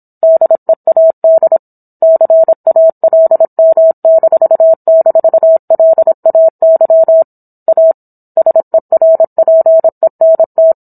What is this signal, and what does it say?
Morse code, heard as a keyed tone, about 27 wpm, 650 Hz. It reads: DEAD CALM--LAY A SERPENT